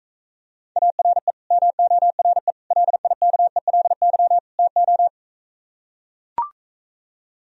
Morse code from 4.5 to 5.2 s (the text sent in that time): TO